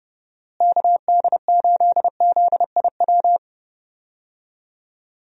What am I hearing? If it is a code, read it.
Morse code, heard as a keyed tone, about 29 wpm, 700 Hz. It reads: KD8ZIW
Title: USA Call Signs - 6 Characters - Repeated 2x - 30wpm